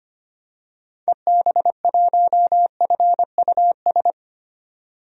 Morse code, heard as a keyed tone, about 25 wpm, 700 Hz. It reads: EB1FUS
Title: DXCC-100 Call Signs - 6 Characters - 25wpm